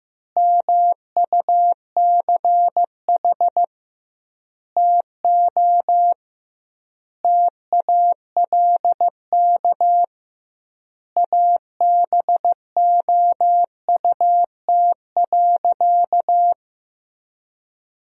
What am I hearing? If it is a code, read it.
Morse code, heard as a keyed tone, about 15 wpm, 700 Hz. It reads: MUCH TO TALK ABOUT.
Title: 2x Word-Spacing - Sentences from Top 400 Words - 15wpm